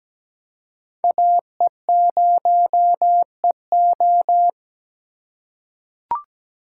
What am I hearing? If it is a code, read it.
Morse code, heard as a keyed tone, about 17 wpm, 700 Hz. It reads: AE0EO
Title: USA Call Signs - 5 Characters - 4x Character Spacing - 17wpm